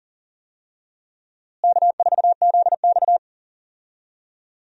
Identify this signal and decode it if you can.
Morse code, heard as a keyed tone, about 39 wpm, 700 Hz. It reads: K4ZX
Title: USA Call Signs - 4 Characters - Repeated 3x - 40wpm